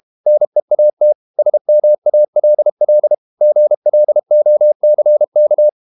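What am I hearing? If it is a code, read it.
Morse code, heard as a keyed tone, about 32 wpm, 600 Hz. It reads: NEAT SMALL GLOCK
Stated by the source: Morse code practice at 32 wpm from CW med Gnister, reading Ken Follett's novel Never in English.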